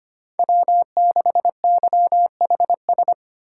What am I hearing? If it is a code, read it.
Morse code, heard as a keyed tone, about 25 wpm, 700 Hz. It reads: W6YHS
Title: USA Call Signs - 5 Characters - 2x Character Spacing - 25wpm